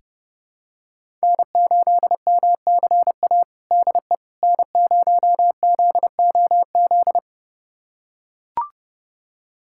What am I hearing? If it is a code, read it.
Morse code, heard as a keyed tone, about 29 wpm, 700 Hz. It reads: N8MCA DE N0ZOZ
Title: USA Callsign DE Callsign - 5 Characters - 30wpm